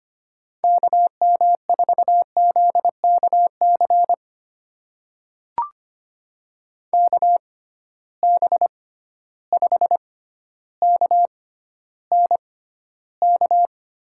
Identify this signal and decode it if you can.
Morse code, heard as a keyed tone, about 25 wpm, spaced as for 13 wpm, 700 Hz. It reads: KM4ZKC KB5KNK
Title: USA Call Signs - 6 Characters - 6x Character Spacing - 25wpm